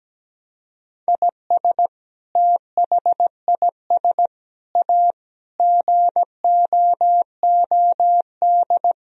Morse code, heard as a keyed tone, about 17 wpm, 700 Hz. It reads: IS THIS A GOOD
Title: Sentences from Top 1000 Words - Review Part 3 - 17wpm